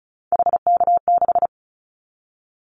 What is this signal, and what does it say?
Morse code, heard as a keyed tone, about 36 wpm, 700 Hz. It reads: HK6
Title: Callsign Prefixes for Top 100 DXCC - 35wpm